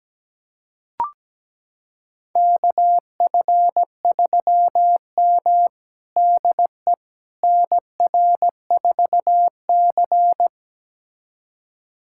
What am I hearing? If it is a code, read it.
Morse code, heard as a keyed tone, about 17 wpm, 700 Hz. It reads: KF3M DE NR4C
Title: USA Callsign DE Callsign - 4 Characters - 17wpm